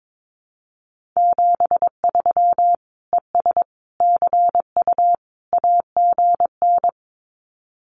Morse code, heard as a keyed tone, about 22 wpm, 700 Hz. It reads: 73 ES CU AGN
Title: Common QSO Phrases Encore 22wpm